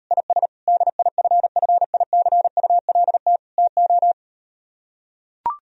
Morse code, heard as a keyed tone, about 38 wpm, 700 Hz. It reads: IS DIFFICULT TO